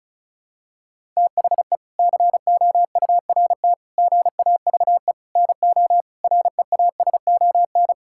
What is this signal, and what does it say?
Morse code, heard as a keyed tone, about 36 wpm, 700 Hz. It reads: THE COURT GAVE NO REASON
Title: Sentences from Top 1600 Words 35wpm